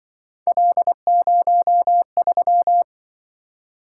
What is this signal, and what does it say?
Morse code, heard as a keyed tone, about 24 wpm, 700 Hz. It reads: L03
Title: Speed-Racing - Three Letter-Number Combinations - 20wpm